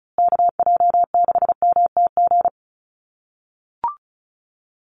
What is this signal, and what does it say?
Morse code, heard as a keyed tone, about 34 wpm, 700 Hz. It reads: KJ6MTG
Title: USA Call Signs Repeated 3x 35wpm